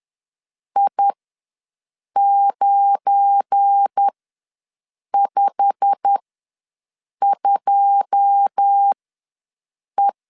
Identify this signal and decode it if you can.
Morse code, heard as a keyed tone, about 10 wpm, spaced as for 6 wpm, 800 Hz. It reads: I952E
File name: LEZIONI / 9 2 ws 10wpm 4130